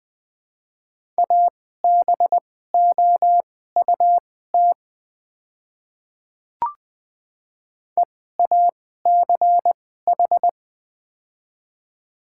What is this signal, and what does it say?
Morse code, heard as a keyed tone, about 20 wpm, spaced as for 14 wpm, 700 Hz. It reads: ABOUT EACH
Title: 2x Character Spacing - Top 100 Words - 20wpm